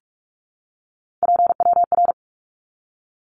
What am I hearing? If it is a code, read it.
Morse code, heard as a keyed tone, about 44 wpm, 700 Hz. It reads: PWR
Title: Abbreviations 45wpm